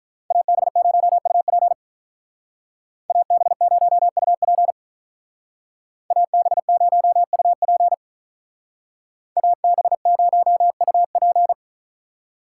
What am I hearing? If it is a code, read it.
Morse code, heard as a keyed tone, about 43 wpm, 700 Hz. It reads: AB0UP AB0UP AB0UP AB0UP
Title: Speed Racing - USA Call Signs 35wpm